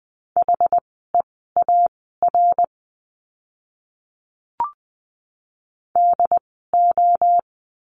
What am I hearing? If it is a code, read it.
Morse code, heard as a keyed tone, about 20 wpm, spaced as for 14 wpm, 700 Hz. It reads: HEAR DO
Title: 206 Lesson 33f - Letter j - Words 20wpm